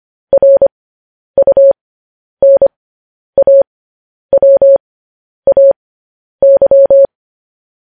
Morse code, heard as a keyed tone, about 25 wpm, spaced as for 10 wpm, 550 Hz. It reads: RUNAWAY